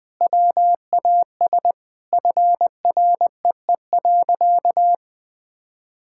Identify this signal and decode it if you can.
Morse code, heard as a keyed tone, about 20 wpm, 700 Hz. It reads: WAS FREE.
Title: Sentences with 4-Word Phrases - 20wpm